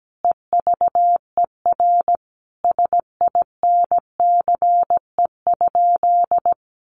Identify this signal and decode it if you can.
Morse code, heard as a keyed tone, about 17 wpm, 700 Hz. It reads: EVER SINCE?
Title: Sentences from Top 700 Words 17wpm